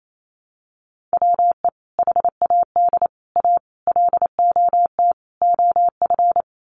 Morse code, heard as a keyed tone, about 28 wpm, 700 Hz. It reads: WE HAD A LOT OF
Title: Sets of 5 Words Encore 28wpm